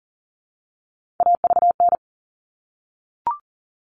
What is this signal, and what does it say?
Morse code, heard as a keyed tone, about 40 wpm, 700 Hz. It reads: AVN
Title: All Cut Numbers - 3-Digit 40wpm